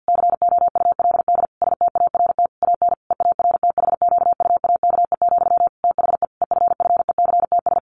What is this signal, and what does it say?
Morse code, heard as a keyed tone, about 50 wpm, 700 Hz. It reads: COULD START AN EARTHQUAKE, THE EFFECTS
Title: A Sound Of Thunder 50wpm